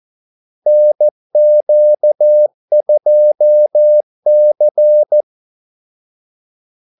Morse code, heard as a keyed tone, about 14 wpm, 600 Hz. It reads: NQ2C
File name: Callsigns-14wpm